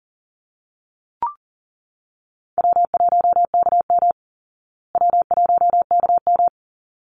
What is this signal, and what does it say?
Morse code, heard as a keyed tone, about 40 wpm, 700 Hz. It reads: W1KM W1KM
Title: Call Signs Repeated 3x - 40wpm